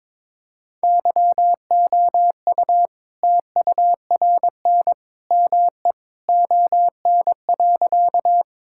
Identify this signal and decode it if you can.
Morse code, heard as a keyed tone, about 22 wpm, 690 Hz. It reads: YOU TURN ME ON.
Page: morse-code-ninja-practice-22wpm